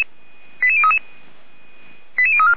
Five-tone selective call (ZVEI), heard as quick runs of five tones, tones of about 70 ms. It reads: 88033 88034